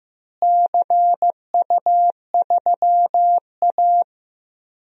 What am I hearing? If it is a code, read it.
Morse code, heard as a keyed tone, about 15 wpm, 700 Hz. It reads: CU3A
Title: DXCC-100 Call Signs - 4 Characters - Repeated 2x - 15wpm